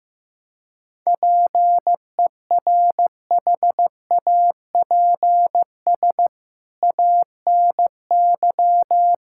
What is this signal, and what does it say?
Morse code, heard as a keyed tone, about 15 wpm, 700 Hz. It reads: PERHAPS ANY
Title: Sentences from Top 1600 Words 15wpm